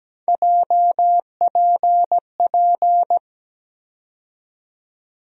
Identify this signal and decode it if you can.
Morse code, heard as a keyed tone, about 17 wpm, 700 Hz. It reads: JPP
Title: USA Call Sign Postfixes - 3 Letters 17wpm